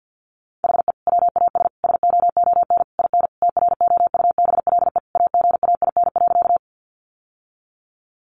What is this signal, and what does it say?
Morse code, heard as a keyed tone, about 48 wpm, 700 Hz. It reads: HE WAS SOON IN TROUBLE AGAIN.
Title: Sentences from Top 1200 Words 50wpm